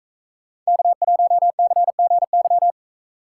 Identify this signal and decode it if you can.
Morse code, heard as a keyed tone, about 44 wpm, 700 Hz. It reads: K1CGY